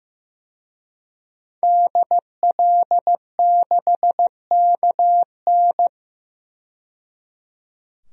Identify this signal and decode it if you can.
Morse code, heard as a keyed tone, about 15 wpm, 700 Hz. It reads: DL6KN
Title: DXCC-100 Call Signs - 5 Characters - 15wpm